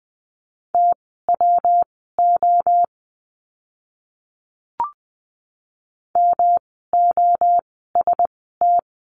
Morse code, heard as a keyed tone, about 20 wpm, spaced as for 15 wpm, 700 Hz. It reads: TWO MOST
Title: 146 Lesson 24f - Letter y - Words 20wpm